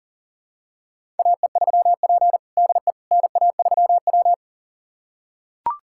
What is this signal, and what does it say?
Morse code, heard as a keyed tone, about 40 wpm, 700 Hz. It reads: AE3P DE NA3W
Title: USA Callsign DE Callsign - 4 Characters - 40wpm